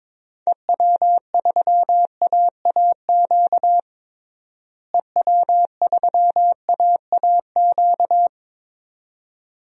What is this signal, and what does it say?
Morse code, heard as a keyed tone, about 22 wpm, 700 Hz. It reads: EW3AAQ EW3AAQ
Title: DXCC-100 Call Signs - 6 Characters - Repeated 2x - 22wpm